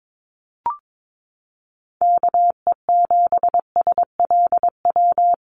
Morse code, heard as a keyed tone, about 22 wpm, 700 Hz. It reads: KE7SLW
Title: USA Call Signs Repeated 2x 22wpm